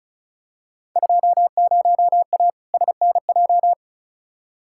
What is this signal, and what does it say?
Morse code, heard as a keyed tone, about 35 wpm, 700 Hz. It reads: 20A SNJ
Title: Field Day Exchanges Encore - 35wpm